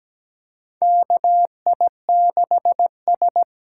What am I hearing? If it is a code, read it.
Morse code, heard as a keyed tone, about 17 wpm, 700 Hz. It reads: K I 6 S
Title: USA Call Signs - 6 Characters - 8x Character Spacing - 17wpm